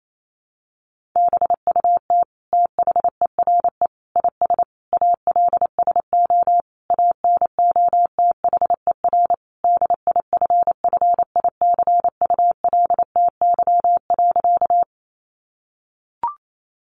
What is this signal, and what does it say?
Morse code, heard as a keyed tone, about 28 wpm, 700 Hz. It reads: BUT THERE IS ALSO ANOTHER DIFFICULTY.
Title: Sentences from Top 1500 Words 28wpm